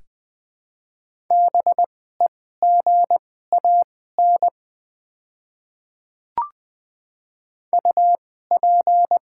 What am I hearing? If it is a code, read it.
Morse code, heard as a keyed tone, about 20 wpm, spaced as for 15 wpm, 690 Hz. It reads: BEGAN UP
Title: 2x Character Spacing - Top 500 Words - 20wpm